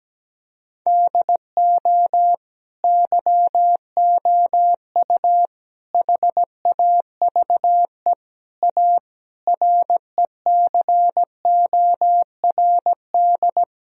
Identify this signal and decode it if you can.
Morse code, heard as a keyed tone, about 17 wpm, 700 Hz. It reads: DO YOU HAVE A RECORD
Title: Sentences from Top 1000 Words - Review Part 3 - 17wpm